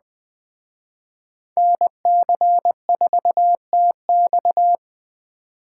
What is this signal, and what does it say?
Morse code, heard as a keyed tone, about 21 wpm, 700 Hz. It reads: NC4TX
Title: Speed Racing - USA 2x2 Call Signs 17wpm